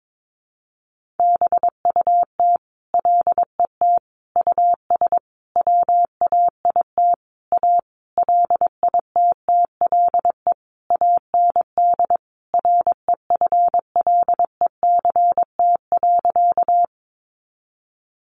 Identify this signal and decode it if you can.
Morse code, heard as a keyed tone, about 22 wpm, 700 Hz. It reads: BUT LET US WAIT A LITTLE AND REFLECT.
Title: Sentences from Top 1500 Words 22wpm